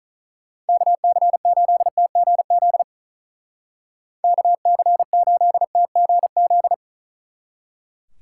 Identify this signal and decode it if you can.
Morse code, heard as a keyed tone, about 45 wpm, 700 Hz. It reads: KC8TGZ KC8TGZ